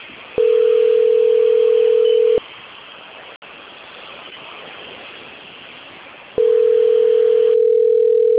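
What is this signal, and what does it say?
Ringback tone (North America), 440 + 480 Hz.
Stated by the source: Outgoing telephone call recording, Sao Carlos, Brazil.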